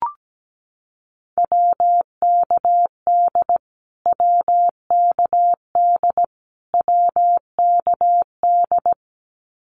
Morse code, heard as a keyed tone, about 17 wpm, 700 Hz. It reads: WKD WKD WKD
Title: Abbreviations Repeated 3x - 17wpm